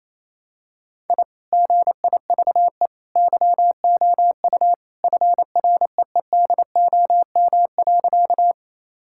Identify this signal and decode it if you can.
Morse code, heard as a keyed tone, about 28 wpm, 700 Hz. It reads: I GIVE YOU FREEDOM.
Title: Sentences from Top 1000 Words - Review Part 5 - 28wpm